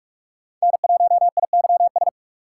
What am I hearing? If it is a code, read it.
Morse code, heard as a keyed tone, about 44 wpm, 700 Hz. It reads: N1IYS